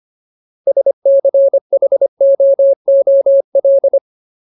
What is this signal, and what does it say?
Morse code, heard as a keyed tone, about 25 wpm, 550 Hz. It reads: SCHOOL
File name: Ultimate Word List / 25WPM Ultimate